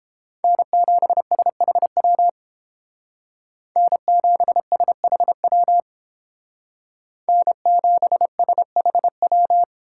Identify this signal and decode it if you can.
Morse code, heard as a keyed tone, about 29 wpm, 700 Hz. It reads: N7SHW N7SHW N7SHW